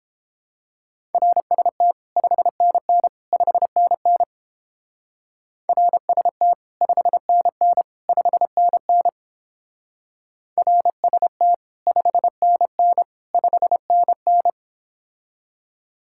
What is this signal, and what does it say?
Morse code, heard as a keyed tone, about 31 wpm, 700 Hz. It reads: RST 5NN 5NN RST 5NN 5NN RST 5NN 5NN